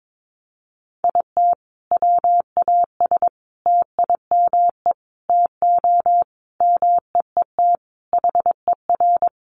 Morse code, heard as a keyed tone, about 22 wpm, 700 Hz. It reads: IT WAS TIME TO MEET HER
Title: Sentences from Top 1100 Words 22wpm